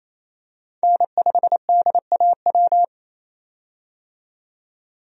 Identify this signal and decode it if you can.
Morse code, heard as a keyed tone, about 28 wpm, 700 Hz. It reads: N5DAW
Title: USA Call Signs - 5 Characters - 28wpm